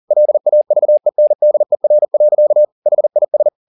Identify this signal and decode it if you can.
Morse code, heard as a keyed tone, about 40 wpm, 600 Hz. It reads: LAVENDER. HIS